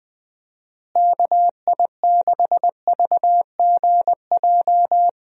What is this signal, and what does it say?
Morse code, heard as a keyed tone, about 21 wpm, 700 Hz. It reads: KI6VGJ
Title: USA Call Signs - 6 Characters - Repeated 2x - 20wpm